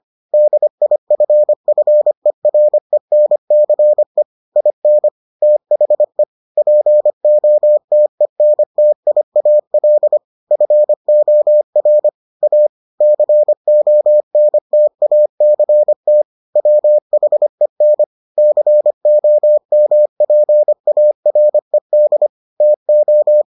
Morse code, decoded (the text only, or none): DIFFERENCE IN THE POTENTIAL FOR A CONTACT WHEN COMPARED TO